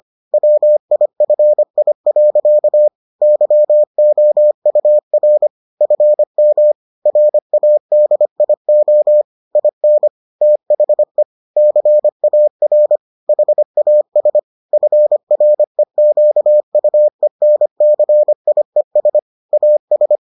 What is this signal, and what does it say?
Morse code, heard as a keyed tone, about 25 wpm, 600 Hz. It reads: WIFI. YOUR FM RADIO IN THE CAR HAS FREQUENCIES AS